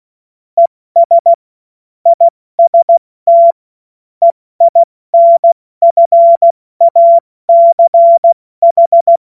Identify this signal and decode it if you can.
Morse code, heard as a keyed tone, about 16 wpm, 680 Hz. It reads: ES IST EINFACH
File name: Morsekurs / Witze / Witz 29. Juni 2021, 70 F80